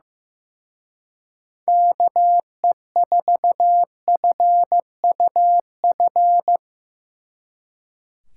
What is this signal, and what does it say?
Morse code, heard as a keyed tone, about 16 wpm, 700 Hz. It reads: KE4FUF
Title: Speed Racing - USA Call Signs 15wpm